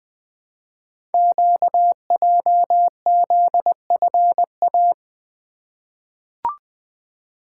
Morse code, heard as a keyed tone, about 20 wpm, 700 Hz. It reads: QJZFA
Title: Code Groups - 5 Letters - 20wpm